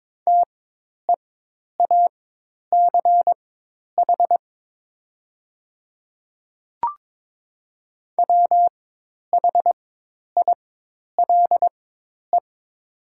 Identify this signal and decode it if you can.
Morse code, heard as a keyed tone, about 22 wpm, spaced as for 10 wpm, 700 Hz. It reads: TEACH WHILE
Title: 4x Character Spacing - Top 500 Words - 22wpm